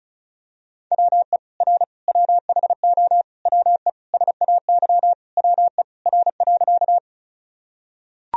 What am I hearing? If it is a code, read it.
Morse code, heard as a keyed tone, about 34 wpm, 700 Hz. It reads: WE R WHO WE SAY WE R.
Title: Sentences from Top 200 Words and Abbreviations 35wpm